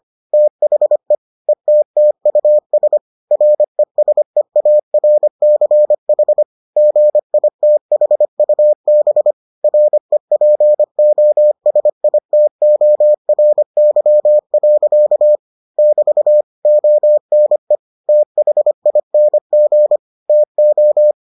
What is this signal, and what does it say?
Morse code, heard as a keyed tone, about 25 wpm, 600 Hz. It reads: THE ETTUS RESEARCH GITHUB REPOSITORY. <BT> ONE THING TO